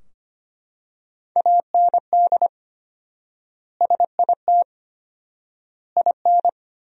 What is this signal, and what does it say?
Morse code, heard as a keyed tone, about 25 wpm, 700 Hz. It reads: AND SIT IN
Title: 4x Word-Spacing - Sentences from Top 500 Words - 25wpm